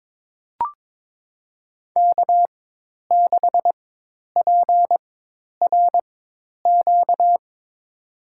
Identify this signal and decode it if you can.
Morse code, heard as a keyed tone, about 21 wpm, spaced as for 10 wpm, 700 Hz. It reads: K6PRQ
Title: USA Call Signs - 5 Characters - 4x Character Spacing - 22wpm